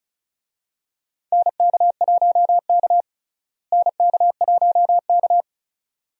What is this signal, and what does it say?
Morse code, heard as a keyed tone, about 34 wpm, 700 Hz. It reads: NK1K NK1K
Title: USA Call Signs - 4 Characters - Repeated 2x - 35wpm